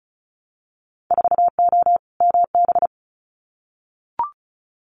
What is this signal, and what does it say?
Morse code, heard as a keyed tone, about 34 wpm, 700 Hz. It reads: 4O MB
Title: Winter Field Day Exchanges Encore 35wpm